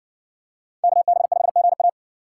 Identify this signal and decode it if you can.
Morse code, heard as a keyed tone, about 57 wpm, 700 Hz. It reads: KB5GA